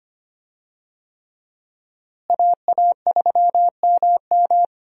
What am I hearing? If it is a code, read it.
Morse code, heard as a keyed tone, about 25 wpm, 700 Hz. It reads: AA3MM
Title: USA Call Signs - 5 Characters - Repeated 3x - 25wpm